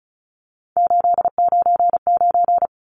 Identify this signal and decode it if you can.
Morse code, heard as a keyed tone, about 34 wpm, 700 Hz. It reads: 899
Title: Numbers - 3-Digit 35wpm